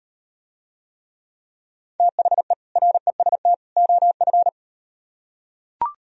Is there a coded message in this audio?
Yes: Morse code from 2.0 to 4.5 s, heard as a keyed tone, about 38 wpm, 700 Hz. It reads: THE REST OF